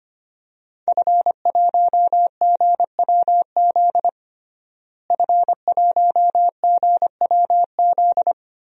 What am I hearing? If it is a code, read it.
Morse code, heard as a keyed tone, about 25 wpm, 700 Hz. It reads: F1GWZ F1GWZ
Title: DXCC-100 Call Signs - 5 Characters - Repeated 2x - 25wpm